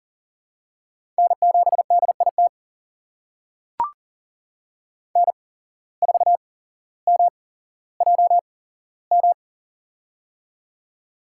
Morse code, heard as a keyed tone, about 40 wpm, spaced as for 17 wpm, 700 Hz. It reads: N7DIT N4MJM
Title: USA Call Signs - 8x Character Spacing - 40wpm